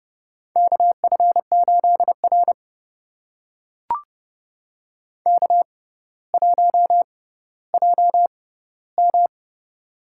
Morse code, heard as a keyed tone, about 32 wpm, spaced as for 16 wpm, 700 Hz. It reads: KF8R K1JM
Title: USA Call Signs - 6x Character Spacing - 30wpm